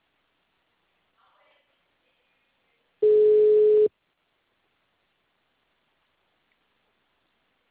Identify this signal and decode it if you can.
Ringback tone (Europe, 425 Hz), with speech.